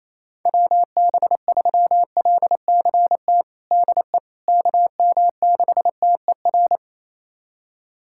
Morse code, heard as a keyed tone, about 28 wpm, 700 Hz. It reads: WB3LCT DE KM6TER
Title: USA Callsign DE Callsign 28wpm